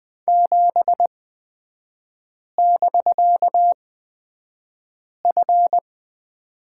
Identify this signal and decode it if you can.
Morse code, heard as a keyed tone, about 20 wpm, spaced as for 5 wpm, 700 Hz. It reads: 7<BK>F
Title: 250 Lesson 40e - BK - Three Characters 20wpm